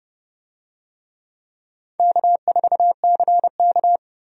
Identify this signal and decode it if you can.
Morse code, heard as a keyed tone, about 30 wpm, 700 Hz. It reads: K4CK
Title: USA Call Signs - 4 Characters - Repeated 2x - 30wpm